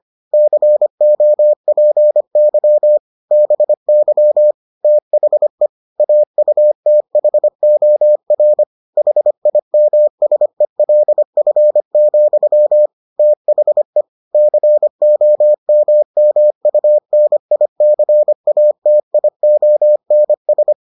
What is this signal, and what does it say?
Morse code, heard as a keyed tone, about 25 wpm, 600 Hz. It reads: COPY BY THE AUTHOR HIMSELF, THE COMMUNICATIONS